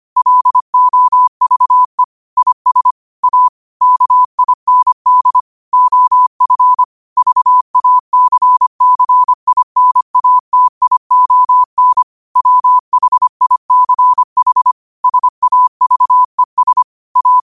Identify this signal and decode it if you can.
Morse code, heard as a keyed tone, about 25 wpm, 1000 Hz. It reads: LOVE IS A KIND OF VACCINATION WHICH SAVES A
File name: qotdfiles / 25wpm